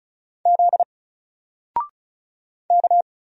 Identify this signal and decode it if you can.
Morse code, heard as a keyed tone, about 36 wpm, 700 Hz. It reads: Z K
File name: Single-Letter-Number-35wpm